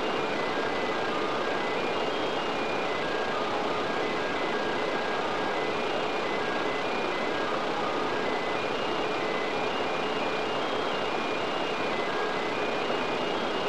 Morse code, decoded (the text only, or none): none